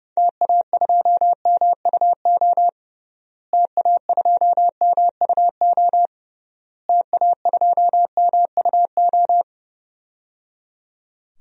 Morse code, heard as a keyed tone, about 29 wpm, 700 Hz. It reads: TA2MUO TA2MUO TA2MUO
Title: DXCC-100 Call Signs - 6 Characters - Repeated 3x - 30wpm